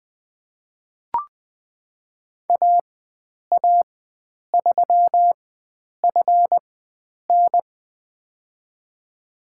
Morse code, heard as a keyed tone, about 20 wpm, spaced as for 9 wpm, 700 Hz. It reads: AA3FN
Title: USA Call Signs - 4x Character Spacing - 20wpm